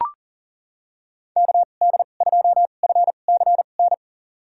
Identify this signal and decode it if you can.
Morse code, heard as a keyed tone, about 40 wpm, spaced as for 29 wpm, 700 Hz. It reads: KD2FCN
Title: USA Call Signs - 6 Characters - 2x Character Spacing - 40wpm